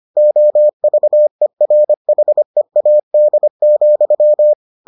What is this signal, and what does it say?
Morse code, heard as a keyed tone, about 25 wpm, 600 Hz. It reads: OVERHEAD,